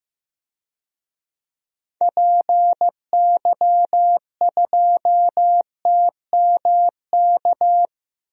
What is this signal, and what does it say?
Morse code, heard as a keyed tone, about 15 wpm, 700 Hz. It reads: PY2TMK
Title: DXCC-100 Call Signs - 6 Characters - 15wpm